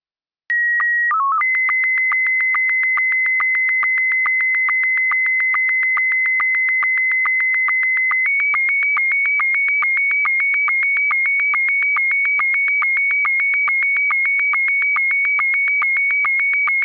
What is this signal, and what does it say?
SSTV (slow-scan television), Scottie 1.
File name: clean